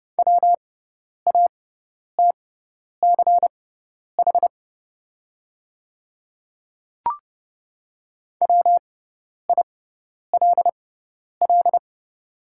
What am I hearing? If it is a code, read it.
Morse code, heard as a keyed tone, about 29 wpm, spaced as for 10 wpm, 700 Hz. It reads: WATCH WILL
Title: 6x Character Spacing - Top 300 Words - 30wpm